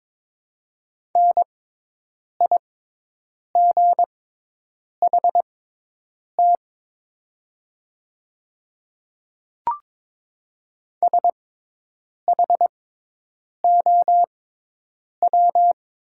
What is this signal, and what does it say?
Morse code, heard as a keyed tone, about 22 wpm, spaced as for 8 wpm, 700 Hz. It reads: NIGHT SHOW
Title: 6x Character Spacing - Top 300 Words - 22wpm